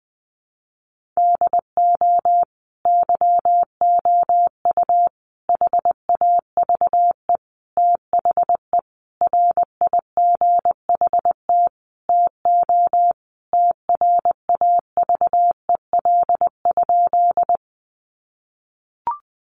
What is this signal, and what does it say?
Morse code, heard as a keyed tone, about 19 wpm, 700 Hz. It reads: DO YOU HAVE THE RIGHT TO TRAVEL?
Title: Sentences from Top 1400 Words 20wpm